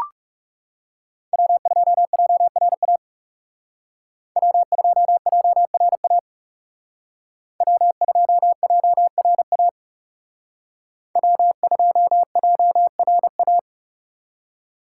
Morse code, heard as a keyed tone, about 38 wpm, 700 Hz. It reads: W2JRA W2JRA W2JRA W2JRA